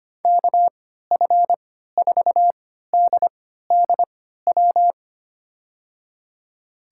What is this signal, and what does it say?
Morse code, heard as a keyed tone, about 25 wpm, spaced as for 14 wpm, 700 Hz. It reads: KF4DDW
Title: USA Call Signs - 6 Characters - 3x Character Spacing - 25wpm